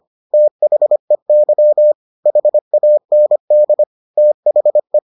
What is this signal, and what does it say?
Morse code, heard as a keyed tone, about 25 wpm, 600 Hz. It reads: THEY HAND THE